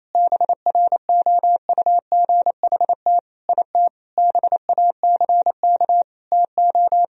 Morse code, heard as a keyed tone, about 28 wpm, 700 Hz. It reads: BROUGHT IT BACK TO
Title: Sentences from Top 1000 Words - Review Part 3 - 28wpm